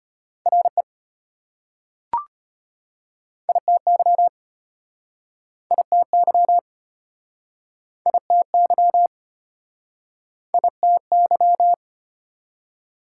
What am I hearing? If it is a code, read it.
Morse code, heard as a keyed tone, about 33 wpm, 700 Hz. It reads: RE ITY ITY ITY ITY